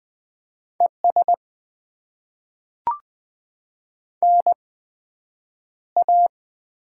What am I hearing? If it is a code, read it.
Morse code, heard as a keyed tone, about 20 wpm, spaced as for 9 wpm, 700 Hz. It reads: ES NA
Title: Top 20 QSO Elements - 8x Character Spacing - 20wpm